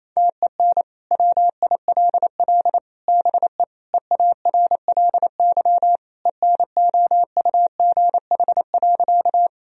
Morse code, heard as a keyed tone, about 27 wpm, 700 Hz. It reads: TEN WILL BE EARLY ENOUGH.
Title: Sentences from Top 1000 Words - Review Part 2 - 28wpm